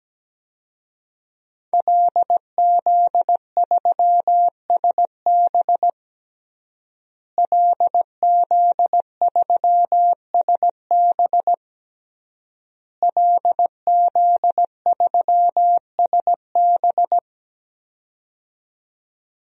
Morse code, heard as a keyed tone, about 17 wpm, 700 Hz. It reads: LZ3SB LZ3SB LZ3SB
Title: DXCC-100 Call Signs - 5 Characters - Repeated 3x - 17wpm